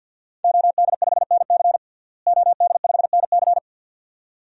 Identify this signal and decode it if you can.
Morse code, heard as a keyed tone, about 47 wpm, 700 Hz. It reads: OD5NC OD5NC
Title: DXCC-100 Call Signs - 5 Characters - Repeated 2x - 50wpm